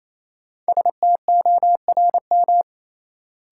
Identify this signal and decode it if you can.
Morse code, heard as a keyed tone, about 28 wpm, 700 Hz. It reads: STORM
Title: Sentences from Top 1500 Words - New Word Drill Alt - 28wpm